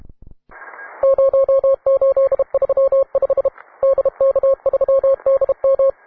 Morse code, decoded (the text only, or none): 0835 DK3DM